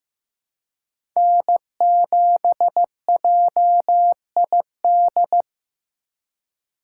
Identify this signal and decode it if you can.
Morse code, heard as a keyed tone, about 15 wpm, 700 Hz. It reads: N7JID
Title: USA Call Signs - 2x Character Spacing - 15wpm